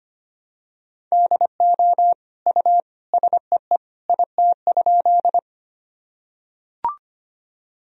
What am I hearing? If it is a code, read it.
Morse code, heard as a keyed tone, about 25 wpm, 700 Hz. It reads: DO U SEE IT?
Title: Sentences from Top 100 Words and Abbreviations 25wpm